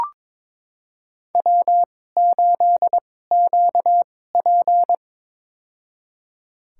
Morse code, heard as a keyed tone, about 22 wpm, spaced as for 16 wpm, 700 Hz. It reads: W8QP
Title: USA Call Signs - 4 Characters - 2x Character Spacing - 22wpm